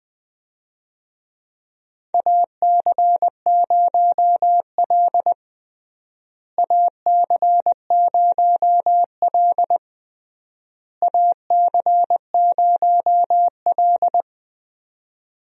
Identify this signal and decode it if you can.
Morse code, heard as a keyed tone, about 20 wpm, 700 Hz. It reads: AC0L AC0L AC0L
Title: USA Call Signs - 4 Characters - Repeated 3x - 20wpm